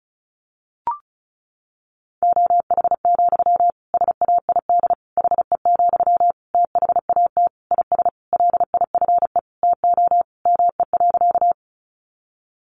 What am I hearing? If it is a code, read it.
Morse code, heard as a keyed tone, about 34 wpm, 700 Hz. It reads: OH, SAID HE, THAT IS LIFE TO ME.
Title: Sentences from Top 500 Words 35wpm